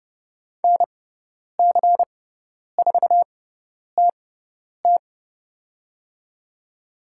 Morse code, heard as a keyed tone, about 30 wpm, spaced as for 10 wpm, 700 Hz. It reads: NC4TT
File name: Lesson 16g - Letter c - Call Signs 30wpm